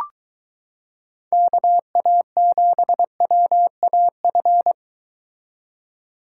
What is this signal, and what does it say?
Morse code, heard as a keyed tone, about 23 wpm, 700 Hz. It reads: KA7WAF